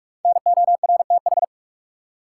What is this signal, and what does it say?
Morse code, heard as a keyed tone, about 43 wpm, 700 Hz. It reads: NORTH
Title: Five Letter Words - 45wpm